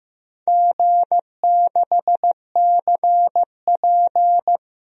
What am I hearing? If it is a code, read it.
Morse code, heard as a keyed tone, about 15 wpm, 700 Hz. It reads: G6CP
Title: DXCC-100 Call Signs - 4 Characters - 15wpm